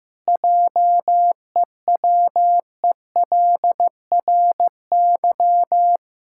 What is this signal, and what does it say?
Morse code, heard as a keyed tone, about 16 wpm, 700 Hz. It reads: JEWELRY
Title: Pangram Challenge v2 - 15wpm